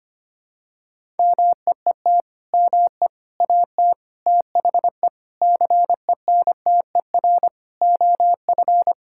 Morse code, heard as a keyed tone, about 25 wpm, 700 Hz. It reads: MEET ME AT THE CENTER OF
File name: practice / 25wpm / Sentences with 3-Word Phrases - 25wpm